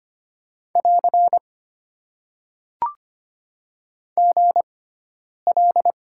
Morse code, heard as a keyed tone, about 25 wpm, spaced as for 9 wpm, 700 Hz. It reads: <AR> GL
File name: Top 90 QSO Elements - 6x Character Spacing - 25wpm